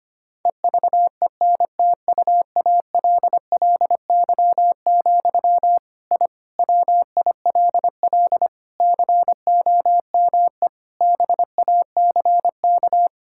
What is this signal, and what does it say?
Morse code, heard as a keyed tone, about 25 wpm, 700 Hz. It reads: EVENTUALLY, I WILL COME BACK
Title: Sentences from Top 1000 Words - Review Part 6 - 25wpm